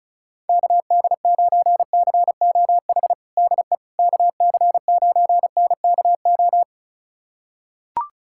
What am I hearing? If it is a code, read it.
Morse code, heard as a keyed tone, about 36 wpm, 700 Hz. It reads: KD9COH DE KC9NKO
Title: USA Callsign DE Callsign - 6 Characters - 35wpm